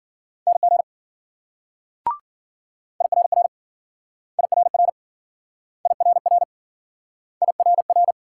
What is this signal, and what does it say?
Morse code, heard as a keyed tone, about 53 wpm, 700 Hz. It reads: NG IRR IRR IRR IRR